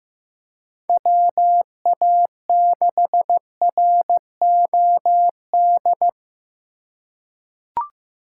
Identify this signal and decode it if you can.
Morse code, heard as a keyed tone, about 15 wpm, 700 Hz. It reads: WA6ROD
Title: USA Call Signs - 2x Character Spacing - 15wpm